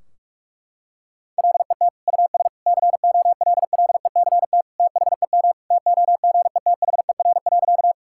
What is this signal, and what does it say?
Morse code, heard as a keyed tone, about 46 wpm, 700 Hz. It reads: LET US COLLECT THEM TOGETHER.
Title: Sentences from Top 1600 Words 45wpm